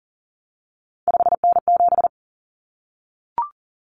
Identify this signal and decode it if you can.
Morse code, heard as a keyed tone, about 40 wpm, 700 Hz. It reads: 5N7
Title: Common Cut Numbers - 3-Digit 40wpm